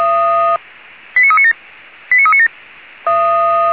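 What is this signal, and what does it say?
Five-tone selective call (ZVEI), heard as quick runs of five tones, tones of about 70 ms. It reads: 89397 89397
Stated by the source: ZVEI five-tone selective call received on an SDR.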